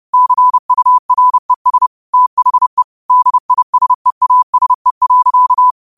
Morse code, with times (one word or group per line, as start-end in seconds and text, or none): 0.1-1.9 CURES
2.1-2.8 THE
3.1-5.7 DISEASE.